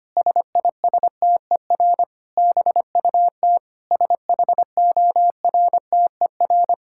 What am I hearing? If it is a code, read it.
Morse code, heard as a keyed tone, about 25 wpm, 700 Hz. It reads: SISTER BUT SHORTER
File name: practice / 25wpm / Difficult Sentences with Hints - 25wpm